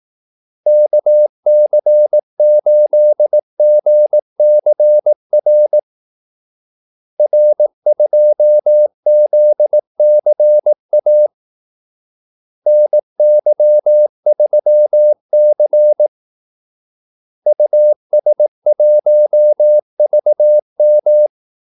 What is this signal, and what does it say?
Morse code, heard as a keyed tone, about 18 wpm, 600 Hz. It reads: KC8GCR R2ZCA NY3C US1VM
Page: Kaldesignaler 18 wpm